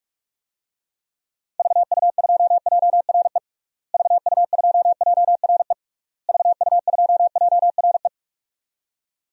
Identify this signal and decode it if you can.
Morse code, heard as a keyed tone, about 46 wpm, 700 Hz. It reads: VU2JRE VU2JRE VU2JRE
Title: DXCC-100 Call Signs - 6 Characters - Repeated 3x - 45wpm